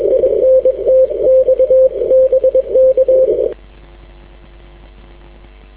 Morse code, heard as a keyed tone, about 22 wpm, 520 Hz. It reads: INT X BK
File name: w5un 160m